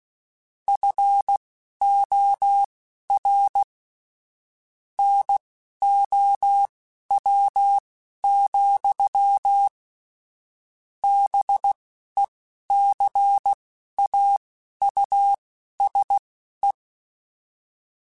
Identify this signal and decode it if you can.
Morse code, heard as a keyed tone, about 16 wpm, spaced as for 11 wpm, 800 Hz. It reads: FOR NOW, BECAUSE
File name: Technical Subjects / Technical Subjects-mp3s / Gates Intro @10